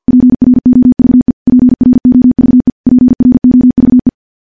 Morse code, heard as a keyed tone, about 46 wpm, 250 Hz. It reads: PROVE PROVE PROVE
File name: morse / morse-training-50WPM